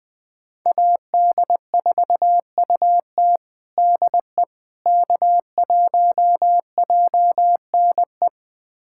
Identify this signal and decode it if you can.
Morse code, heard as a keyed tone, about 21 wpm, 700 Hz. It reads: AD4UT DE K1JNE
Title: USA Callsign DE Callsign - 5 Characters - 20wpm